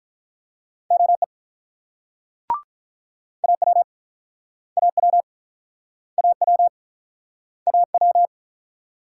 Morse code, heard as a keyed tone, about 48 wpm, 700 Hz. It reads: OE AW AW AW AW